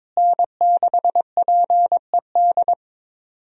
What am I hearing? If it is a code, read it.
Morse code, heard as a keyed tone, about 22 wpm, 700 Hz. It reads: N6PED
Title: USA Call Signs - 5 Characters - Repeated 2x - 22wpm